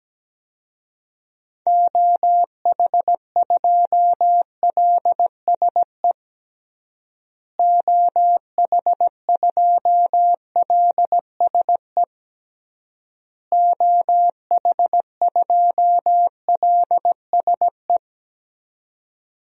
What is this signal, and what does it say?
Morse code, heard as a keyed tone, about 17 wpm, 700 Hz. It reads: OH2LSE OH2LSE OH2LSE
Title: DXCC-100 Call Signs - 6 Characters - Repeated 3x - 17wpm